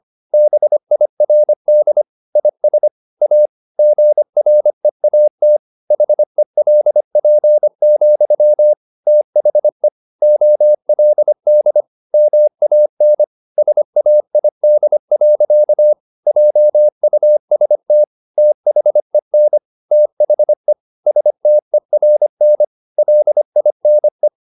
Morse code, heard as a keyed tone, about 25 wpm, 600 Hz. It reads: BIRD IS A GREAT HELP, THE OLD MAN SAID. JUST THEN THE STERN LINE
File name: the old man and the sea part 1 25wpm